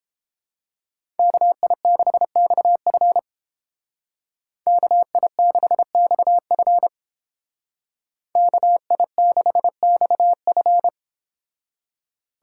Morse code, heard as a keyed tone, about 30 wpm, 700 Hz. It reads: KI6XF KI6XF KI6XF